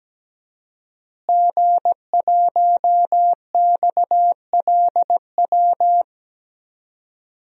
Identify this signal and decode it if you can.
Morse code, heard as a keyed tone, about 17 wpm, 700 Hz. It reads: G1XLW
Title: DXCC-100 Call Signs - 5 Characters - Repeated 2x - 17wpm